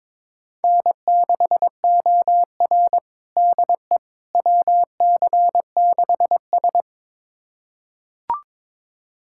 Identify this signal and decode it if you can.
Morse code, heard as a keyed tone, about 22 wpm, 700 Hz. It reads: N6OR DE WC6S
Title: USA Callsign DE Callsign - 4 Characters - 22wpm